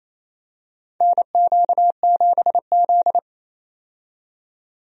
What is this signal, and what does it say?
Morse code, heard as a keyed tone, about 27 wpm, 700 Hz. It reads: NQ7Z
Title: USA Call Signs - 4 Characters - 28wpm